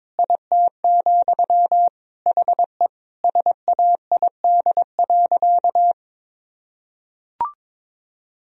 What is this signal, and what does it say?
Morse code, heard as a keyed tone, about 21 wpm, 700 Hz. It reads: IT, HE SAID.
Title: Sentences from Top 1000 Words - Review Part 4 - 22wpm